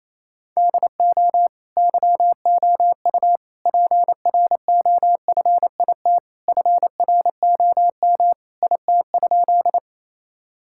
Morse code, heard as a keyed tone, about 28 wpm, 700 Hz. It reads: DO YOU PROFIT FROM IT?
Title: Sentences from Top 1000 Words - Review Part 5 - 28wpm